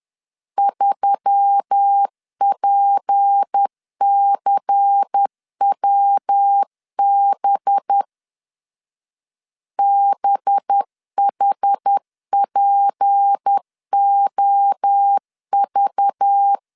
Morse code, heard as a keyed tone, about 10 wpm, 800 Hz. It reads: 3PCWB BHPOV
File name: LEZIONI / 6 PKX0 / 6 PKX0 10wpm 2707